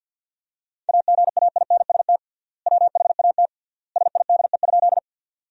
Morse code, heard as a keyed tone, about 50 wpm, 700 Hz. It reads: AGAINST WHAT SIDE?
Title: 3x Word-Spacing - Sentences from Top 400 Words - 50wpm